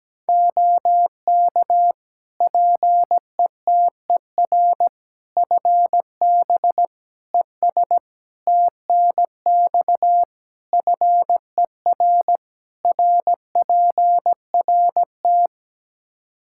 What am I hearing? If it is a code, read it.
Morse code, heard as a keyed tone, about 17 wpm, 700 Hz. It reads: OK PETER FB ES TNX FER RPRT
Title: K1USN SST QSO Practice 17wpm